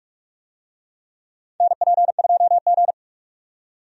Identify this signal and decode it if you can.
Morse code, heard as a keyed tone, about 46 wpm, 700 Hz. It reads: NP2G
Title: USA Call Signs - 4 Characters - 45wpm